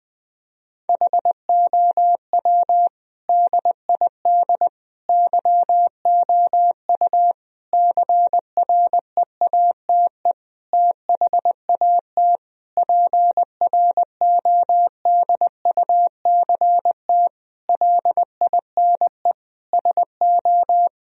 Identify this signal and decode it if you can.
Morse code, heard as a keyed tone, about 21 wpm, 700 Hz. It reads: HOW DID YOU CREATE THAT PRODUCT LINE SO
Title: Sentences from Top 1000 Words - Review Part 3 - 20wpm